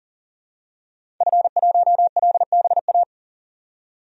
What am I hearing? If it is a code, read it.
Morse code, heard as a keyed tone, about 40 wpm, 700 Hz. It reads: F1LBA
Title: DXCC-100 Call Signs - 5 Characters - 40wpm